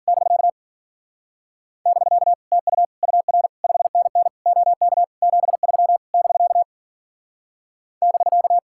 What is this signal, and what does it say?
Morse code, heard as a keyed tone, about 52 wpm, 700 Hz. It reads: <BK> <BK> TU UR 5NN OK 73 <BK> <BK>